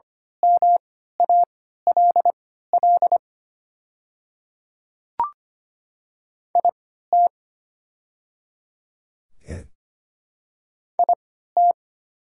Morse code, heard as a keyed tone, about 25 wpm, spaced as for 14 wpm, 690 Hz. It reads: MALL IT IT
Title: 3x Character Spacing - Top 200 Words - 25wpm